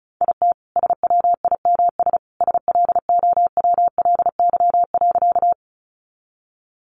Morse code, heard as a keyed tone, about 34 wpm, 700 Hz. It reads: IT SWIMS SLOWLY.
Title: Easy Phrases 35wpm